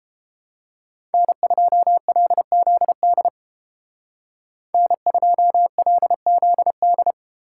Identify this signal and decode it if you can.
Morse code, heard as a keyed tone, about 28 wpm, 700 Hz. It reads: N2LZD N2LZD